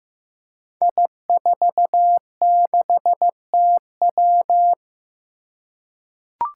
Morse code, heard as a keyed tone, about 15 wpm, 700 Hz. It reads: I46TW
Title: Five Challenging Letter-Number Combinations 15wpm